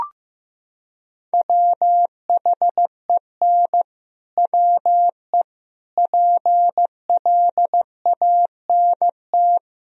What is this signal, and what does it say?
Morse code, heard as a keyed tone, about 15 wpm, 700 Hz. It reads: WHEN WE PLANT
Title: Sentences from Top 1000 Words - Review Part 1 - 15wpm